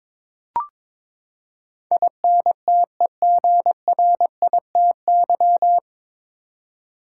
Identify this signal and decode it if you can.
Morse code, heard as a keyed tone, about 22 wpm, 700 Hz. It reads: INTEGRITY
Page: Morse Code Ninja practice 22wpm